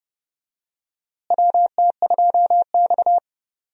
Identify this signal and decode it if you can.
Morse code, heard as a keyed tone, about 32 wpm, 700 Hz. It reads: WT2X
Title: USA Call Signs - 4 Characters - Repeated 2x - 30wpm